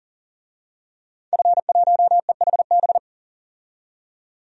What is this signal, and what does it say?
Morse code, heard as a keyed tone, about 40 wpm, 700 Hz. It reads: F1EHB
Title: DXCC-100 Call Signs - 5 Characters - 40wpm